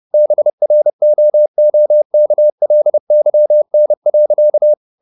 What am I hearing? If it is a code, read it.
Morse code, heard as a keyed tone, about 30 wpm, 600 Hz. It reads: BROOKLYN.
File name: the old man and the sea / the old man and the sea 1 30wpm